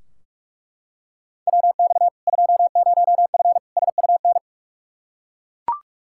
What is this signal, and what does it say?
Morse code, heard as a keyed tone, about 41 wpm, 700 Hz. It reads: WX 20F SUN
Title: Practice QSO - Exchange 2 - Info - 45wpm